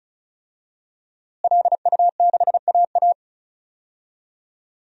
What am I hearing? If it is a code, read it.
Morse code, heard as a keyed tone, about 36 wpm, 700 Hz. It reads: LU6AA
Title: DXCC-100 Call Signs - 5 Characters - Repeated 3x - 35wpm